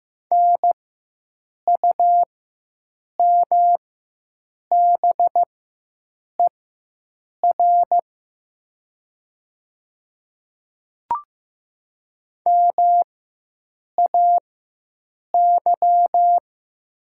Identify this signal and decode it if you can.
Morse code, heard as a keyed tone, about 15 wpm, spaced as for 7 wpm, 700 Hz. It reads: NUMBER MAY
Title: 4x Character Spacing - Top 100 Words - 15wpm